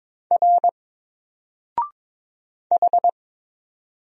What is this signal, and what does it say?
Morse code, heard as a keyed tone, about 22 wpm, 700 Hz. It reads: R H
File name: Single-Letter-Number-22wpm